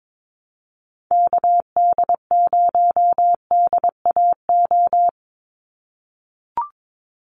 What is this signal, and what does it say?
Morse code, heard as a keyed tone, about 22 wpm, 700 Hz. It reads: KD0DAO
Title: USA Call Signs 22wpm